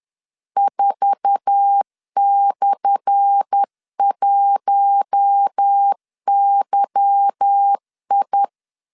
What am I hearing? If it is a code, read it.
Morse code, heard as a keyed tone, about 10 wpm, 800 Hz. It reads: 4/1YI
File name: LEZIONI / 10 SIGNS / 10A SLASH 10wpm 2808